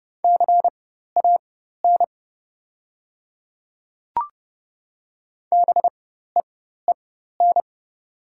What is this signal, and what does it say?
Morse code, heard as a keyed tone, about 30 wpm, spaced as for 14 wpm, 700 Hz. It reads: CAN BEEN
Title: 4x Character Spacing - Top 100 Words - 30wpm